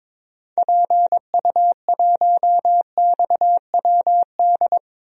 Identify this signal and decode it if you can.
Morse code, heard as a keyed tone, about 22 wpm, 700 Hz. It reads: PU1XWD
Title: DXCC-100 Call Signs - 6 Characters - Repeated 2x - 22wpm